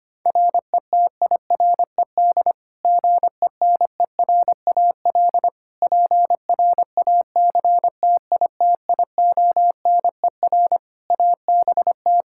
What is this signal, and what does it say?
Morse code, heard as a keyed tone, about 25 wpm, 700 Hz. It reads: RETIRED GENERAL PRACTITIONER ABT
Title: Practice QSO - Exchange 3 - More Info - 25wpm